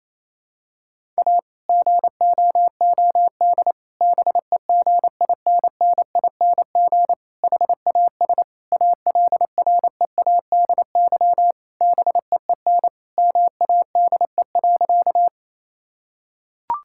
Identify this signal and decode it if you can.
Morse code, heard as a keyed tone, about 28 wpm, 700 Hz. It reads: A GOOD BEGINNING HAS ALREADY BEEN MADE.
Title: Sentences from Top 1000 Words - Review Part 5 - 28wpm